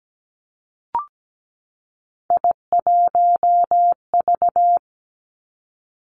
Morse code, heard as a keyed tone, about 17 wpm, 700 Hz. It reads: I1V
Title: Three Challenging Letter-Number Combinations 17wpm